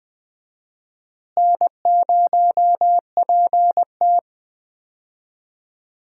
Morse code, heard as a keyed tone, about 20 wpm, 700 Hz. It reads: N0PT
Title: USA Call Signs - 4 Characters - Repeated 3x - 20wpm